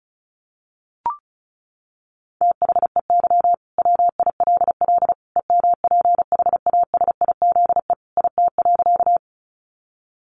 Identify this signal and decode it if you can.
Morse code, heard as a keyed tone, about 34 wpm, 700 Hz. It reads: THEY WILL EMPHASIZE IT.
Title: Sentences from Top 1500 Words 35wpm